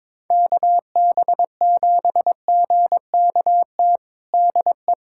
Morse code, heard as a keyed tone, about 22 wpm, 700 Hz. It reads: KB7GKT DE
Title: USA Callsign DE Callsign 22wpm